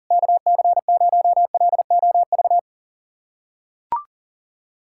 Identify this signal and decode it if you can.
Morse code, heard as a keyed tone, about 40 wpm, 700 Hz. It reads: KC0LOV
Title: SOTA QSO Practice 40wpm